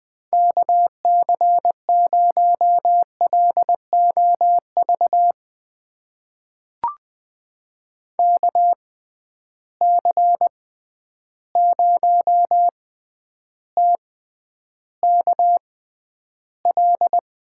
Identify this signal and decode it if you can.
Morse code, heard as a keyed tone, about 20 wpm, spaced as for 10 wpm, 700 Hz. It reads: KC0LOV KC0TKL
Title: USA Call Signs - 6 Characters - 6x Character Spacing - 20wpm